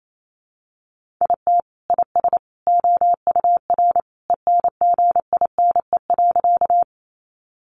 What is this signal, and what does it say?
Morse code, heard as a keyed tone, about 28 wpm, 700 Hz. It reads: IT IS OUR ENGINE.
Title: Sentences from Top 1500 Words 28wpm